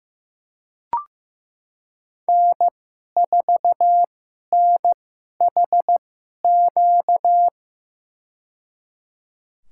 Morse code, heard as a keyed tone, about 15 wpm, spaced as for 11 wpm, 700 Hz. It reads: N4NHQ
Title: USA Call Signs - 5 Characters - 2x Character Spacing - 15wpm